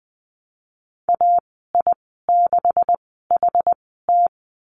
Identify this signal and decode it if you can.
Morse code, heard as a keyed tone, about 20 wpm, spaced as for 15 wpm, 700 Hz. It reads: AI6HT
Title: 154 Lesson 25g - Letter p - Call Signs 20wpm